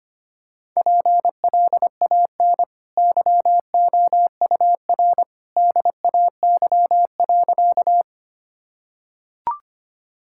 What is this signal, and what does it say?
Morse code, heard as a keyed tone, about 25 wpm, 700 Hz. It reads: PLAN YOUR DAY.
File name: Easy Sentences with Hints - 25wpm